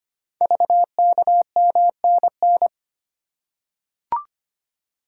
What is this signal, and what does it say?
Morse code, heard as a keyed tone, about 25 wpm, 700 Hz. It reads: VKMNN
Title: Code Groups - 5 Letters - 25wpm